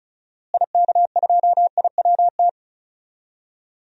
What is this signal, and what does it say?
Morse code, heard as a keyed tone, about 36 wpm, 700 Hz. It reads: IK2IWT
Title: DXCC-100 Call Signs - 6 Characters - Repeated 3x - 35wpm